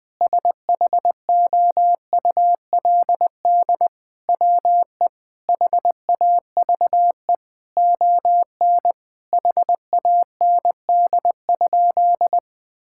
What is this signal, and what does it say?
Morse code, heard as a keyed tone, about 21 wpm, 700 Hz. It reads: SHOULD WE HAVE ON HAND?
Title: Sentences from Top 1000 Words - Review Part 5 - 20wpm